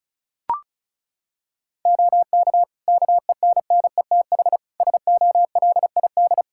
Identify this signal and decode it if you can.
Morse code, heard as a keyed tone, about 36 wpm, 700 Hz. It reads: OK KENNETH SOLID
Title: Practice QSO - Exchange 3 - More Info - 35wpm